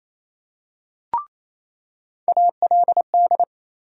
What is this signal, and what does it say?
Morse code, heard as a keyed tone, about 28 wpm, 700 Hz. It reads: ALD
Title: USA Call Sign Postfixes - 3 Letters 28wpm